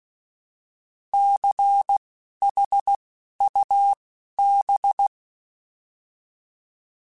Morse code, heard as a keyed tone, about 16 wpm, spaced as for 11 wpm, 800 Hz. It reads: CHUB
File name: Alphabetical / Alphabetical mp3s / Cs @10